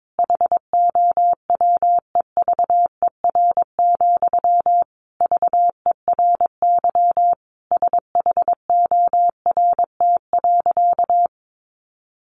Morse code, heard as a keyed tone, about 22 wpm, 700 Hz. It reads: HOWEVER, VERY SHORT.
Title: Sentences from Top 1400 Words 22wpm